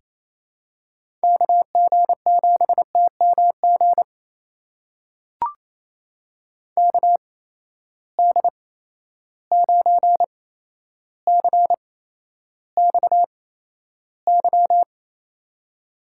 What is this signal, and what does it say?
Morse code, heard as a keyed tone, about 28 wpm, spaced as for 12 wpm, 700 Hz. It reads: KG7TMG KD9CXY
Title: USA Call Signs - 8x Character Spacing - 28wpm